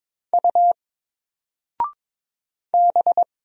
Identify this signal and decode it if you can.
Morse code, heard as a keyed tone, about 22 wpm, 700 Hz. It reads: U B
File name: Single-Letter-Number-22wpm